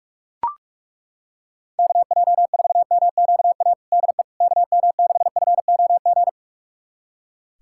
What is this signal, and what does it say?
Morse code, heard as a keyed tone, about 41 wpm, 700 Hz. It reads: KJ4MQA DE KM6FOG
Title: USA Callsign DE Callsign - 6 Characters - 45wpm